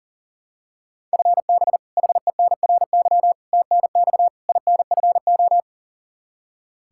Morse code, heard as a keyed tone, about 37 wpm, 700 Hz. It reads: FB HENRY TNX INFO
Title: Practice QSO - Exchange 3 - More Info - 40wpm